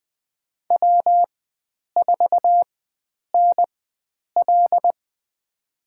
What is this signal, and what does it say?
Morse code, heard as a keyed tone, about 20 wpm, spaced as for 9 wpm, 700 Hz. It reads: W4NL
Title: USA Call Signs - 4 Characters - 4x Character Spacing - 20wpm